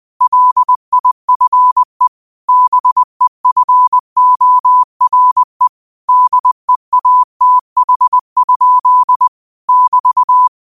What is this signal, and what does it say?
Morse code, heard as a keyed tone, about 20 wpm, 1000 Hz. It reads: LIFE BEFORE DEATH? <BT>